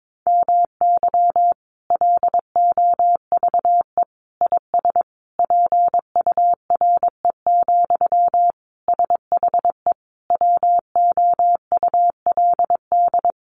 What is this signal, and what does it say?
Morse code, heard as a keyed tone, about 22 wpm, 700 Hz. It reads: MY LOVE IS PURE, SHE WOULD
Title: Sentences from Top 1500 Words 22wpm